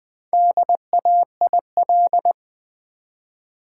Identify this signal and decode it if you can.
Morse code, heard as a keyed tone, about 20 wpm, 700 Hz. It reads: DAIL
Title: Build A Word - 1 Letter At A Time - 5-Letter Words - 20wpm